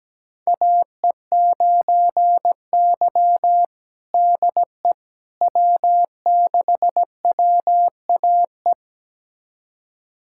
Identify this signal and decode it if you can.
Morse code, heard as a keyed tone, about 17 wpm, 700 Hz. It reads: AE9Y DE W6WAE
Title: USA Callsign DE Callsign 17wpm